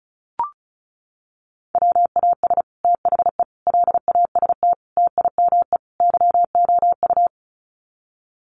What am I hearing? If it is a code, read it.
Morse code, heard as a keyed tone, about 34 wpm, 700 Hz. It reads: WAS THE LAST TIME YOU
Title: Sets of 5 Words Encore2 35wpm